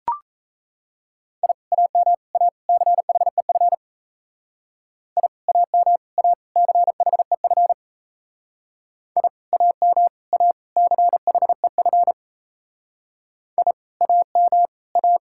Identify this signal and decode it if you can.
Morse code, heard as a keyed tone, about 35 wpm, 700 Hz. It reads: I AM A CHEF I AM A CHEF I AM A CHEF I AM A
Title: Speed-Racing Common QSO Phrases - 28wpm